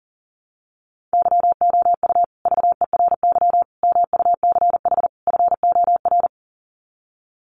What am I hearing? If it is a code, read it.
Morse code, heard as a keyed tone, about 40 wpm, 700 Hz. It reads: YOU VERY MUCH FOR
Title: Sets of 4 Words 40wpm